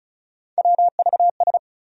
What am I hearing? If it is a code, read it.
Morse code, heard as a keyed tone, about 36 wpm, 700 Hz. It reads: WVS
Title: USA Call Sign Postfixes - 3 Letters 35wpm